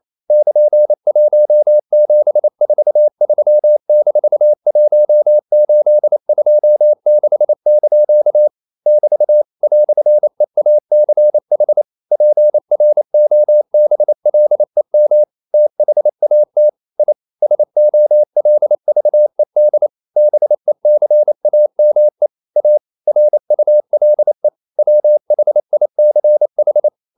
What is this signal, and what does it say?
Morse code, heard as a keyed tone, about 28 wpm, 600 Hz. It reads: <KN>1743-1826) <BT> "EACH PROBLEM THAT I SOLVED BECAME A RULE WHICH
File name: CW / Practice / EnglishQuotations0128